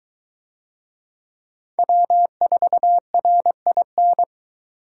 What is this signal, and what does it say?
Morse code, heard as a keyed tone, about 20 wpm, 700 Hz. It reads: W4RIN